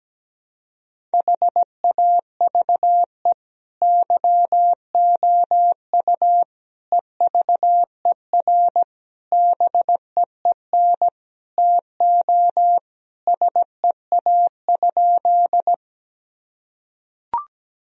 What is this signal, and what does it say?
Morse code, heard as a keyed tone, about 17 wpm, 700 Hz. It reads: HAVE YOU EVER BEEN TO SEA?
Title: Sentences from Top 1000 Words - Review Part 1 - 17wpm